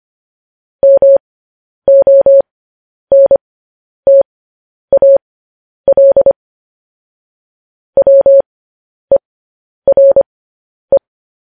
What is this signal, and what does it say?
Morse code, heard as a keyed tone, about 25 wpm, spaced as for 10 wpm, 550 Hz. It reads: MONTAL WERE
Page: news headlines in morse code at 10 wpm